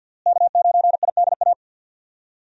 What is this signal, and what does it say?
Morse code, heard as a keyed tone, about 51 wpm, 700 Hz. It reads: K9IDA
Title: USA Call Signs - 5 Characters - Repeated 3x - 50wpm